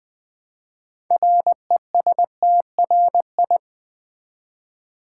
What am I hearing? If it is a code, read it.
Morse code, heard as a keyed tone, about 21 wpm, 700 Hz. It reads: RESTRI
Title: Build A Word - 3 Letters At A Time - 8-Letter Words - 20wpm